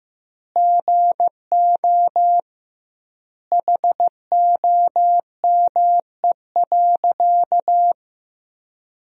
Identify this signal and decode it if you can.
Morse code, heard as a keyed tone, about 15 wpm, 700 Hz. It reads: GO HOME.
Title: 2x Word-Spacing - Sentences from Top 300 Words - 15wpm